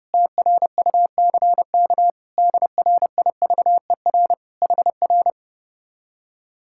Morse code, heard as a keyed tone, about 30 wpm, 700 Hz. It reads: TRUCK DRIVER HR
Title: Speed-Racing Common QSO Phrases - 30wpm